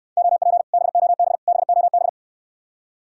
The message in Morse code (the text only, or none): OP BOB BOB